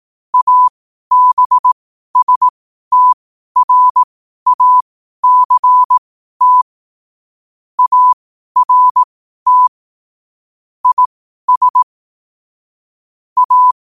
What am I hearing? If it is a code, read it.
Morse code, heard as a keyed tone, about 18 wpm, spaced as for 12 wpm, 1000 Hz. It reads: ABSTRACT ART IS A
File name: qotdfiles / 12wpm